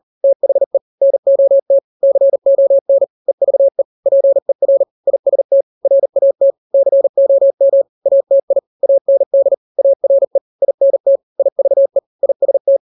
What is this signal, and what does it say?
Morse code, heard as a keyed tone, about 40 wpm, 550 Hz. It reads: THE NOT CON EVE PER IST RAT COM ATI AND ARE INT IVE IST
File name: n-grams / 3-grams 40 35 WPM